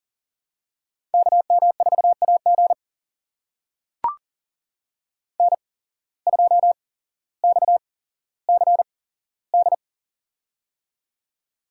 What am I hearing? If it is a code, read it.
Morse code, heard as a keyed tone, about 38 wpm, spaced as for 17 wpm, 700 Hz. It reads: KM4AG N2XCD
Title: USA Call Signs - 5 Characters - 8x Character Spacing - 40wpm